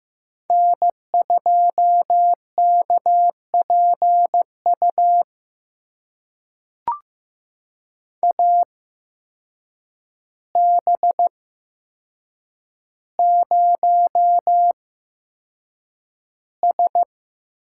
Morse code, heard as a keyed tone, about 15 wpm, spaced as for 7 wpm, 700 Hz. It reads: N2KPU AB0S
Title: USA Call Signs - 5 Characters - 8x Character Spacing - 15wpm